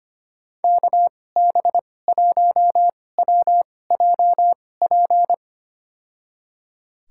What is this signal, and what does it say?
Morse code, heard as a keyed tone, about 25 wpm, spaced as for 18 wpm, 700 Hz. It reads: KB1WJP
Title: USA Call Signs - 2x Character Spacing - 25wpm